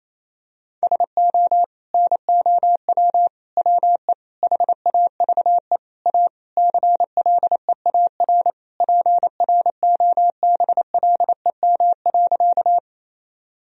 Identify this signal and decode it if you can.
Morse code, heard as a keyed tone, about 28 wpm, 700 Hz. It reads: SO NOW WE HAVE A CLEAR PROBLEM.
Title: Sentences from Top 1000 Words - Review Part 2 - 28wpm